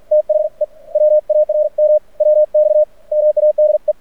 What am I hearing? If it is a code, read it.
Morse code, heard as a keyed tone, about 46 wpm, 610 Hz. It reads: THE WARM WX MADE
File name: archived / CRSnet